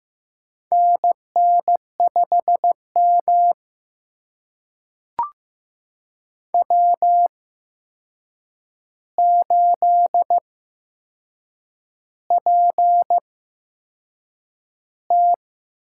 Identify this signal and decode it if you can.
Morse code, heard as a keyed tone, about 15 wpm, spaced as for 6 wpm, 700 Hz. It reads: NN5M W8PT
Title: USA Call Signs - 4 Characters - 8x Character Spacing - 15wpm